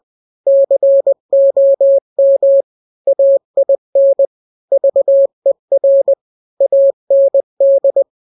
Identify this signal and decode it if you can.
Morse code, heard as a keyed tone, about 20 wpm, 550 Hz. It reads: COM AIN VER AND
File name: n-grams / 3-grams 20 18 WPM